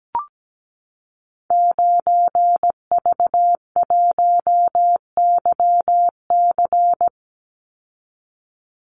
Morse code, heard as a keyed tone, about 17 wpm, 700 Hz. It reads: 9V1YC
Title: Call Signs - 17wpm